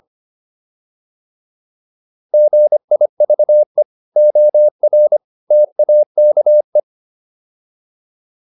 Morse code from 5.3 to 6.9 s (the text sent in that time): TAKE